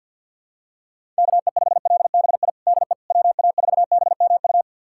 Morse code, heard as a keyed tone, about 49 wpm, 700 Hz. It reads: KE5LBI DE WA4BMU
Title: USA Callsign DE Callsign - 6 Characters - 50wpm